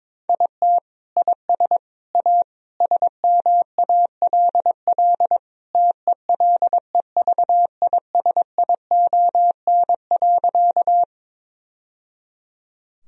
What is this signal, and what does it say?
Morse code, heard as a keyed tone, about 22 wpm, 700 Hz. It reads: IT IS A SMALL TELEVISION.
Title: Sentences from Top 1000 Words - Review Part 4 - 22wpm